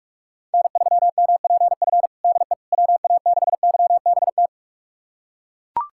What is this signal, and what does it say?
Morse code, heard as a keyed tone, about 41 wpm, 700 Hz. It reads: N3MPF DE WA6YBT
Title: USA Callsign DE Callsign 45wpm